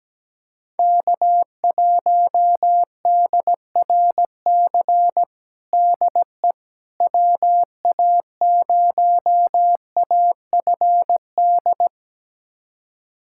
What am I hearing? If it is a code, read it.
Morse code, heard as a keyed tone, about 17 wpm, 700 Hz. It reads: K1DRC DE WA0AFD
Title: USA Callsign DE Callsign 17wpm